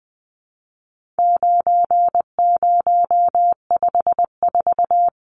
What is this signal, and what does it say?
Morse code, heard as a keyed tone, about 20 wpm, 700 Hz. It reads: 9054
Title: Numbers - 4-Digit 20wpm